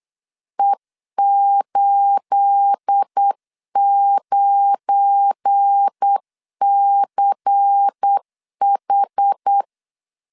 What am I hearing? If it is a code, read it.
Morse code, heard as a keyed tone, about 8 wpm, 800 Hz. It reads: E89CH
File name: LEZIONI / 9 28 8wpm 3512